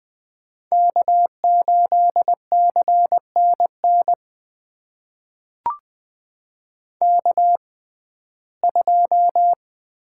Morse code, heard as a keyed tone, about 20 wpm, spaced as for 14 wpm, 700 Hz. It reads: K8CNN K2
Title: USA Call Signs - 5 Characters - 6x Character Spacing - 20wpm